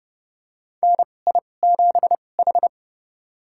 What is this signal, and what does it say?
Morse code, heard as a keyed tone, about 30 wpm, spaced as for 21 wpm, 700 Hz. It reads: NI7H
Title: USA Call Signs - 4 Characters - 2x Character Spacing - 30wpm